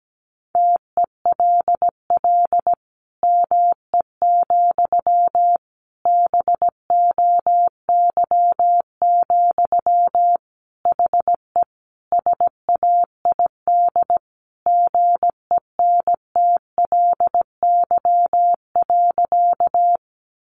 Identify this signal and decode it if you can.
Morse code, heard as a keyed tone, about 17 wpm, 700 Hz. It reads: TELL ME, BOY, HE SAID GENTLY.
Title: Sentences from Top 1500 Words 17wpm